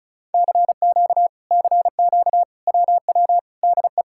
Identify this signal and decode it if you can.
Morse code, heard as a keyed tone, about 38 wpm, 700 Hz. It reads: CQ CQ WW DE
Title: Speed-Racing Common QSO Phrases - 30wpm